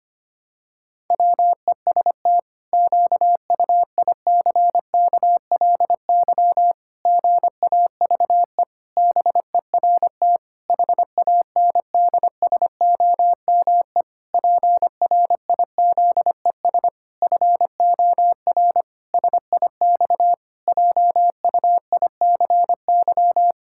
Morse code, heard as a keyed tone, about 25 wpm, 700 Hz. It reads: WEST QUICKLY GAVE BERT HANDSOME PRIZES FOR SIX JUICY
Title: Pangram Challenge v2 - 25wpm